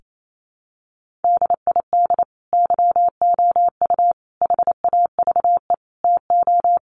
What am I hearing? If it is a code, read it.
Morse code, heard as a keyed tone, about 28 wpm, 700 Hz. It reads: DID YOU HAVE TO
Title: Sentences from Top 1500 Words 28wpm